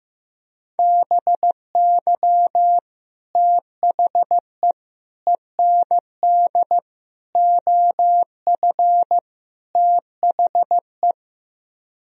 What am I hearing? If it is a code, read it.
Morse code, heard as a keyed tone, about 15 wpm, 700 Hz. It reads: BY THE END OF THE
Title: Speed-Racing - Sets of 5 Words 15wpm